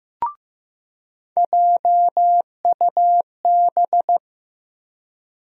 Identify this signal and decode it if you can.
Morse code, heard as a keyed tone, about 15 wpm, 700 Hz. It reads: JUB
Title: USA Call Sign Postfixes - 3 Letters 15wpm